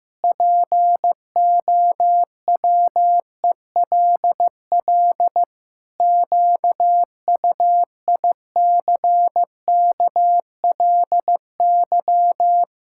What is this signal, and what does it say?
Morse code, heard as a keyed tone, about 15 wpm, 700 Hz. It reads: POWELL QUICKLY
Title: Pangram Challenge v2 - 15wpm